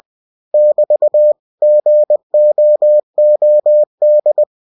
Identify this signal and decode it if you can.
Morse code, heard as a keyed tone, about 20 wpm, 600 Hz. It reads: <BT> GOOD
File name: cw-books / winnie-the-pooh / winnie-20wpm-page-0038